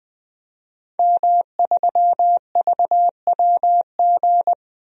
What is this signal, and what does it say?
Morse code, heard as a keyed tone, about 20 wpm, 700 Hz. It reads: M3VWG
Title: DXCC-100 Call Signs - 5 Characters - 20wpm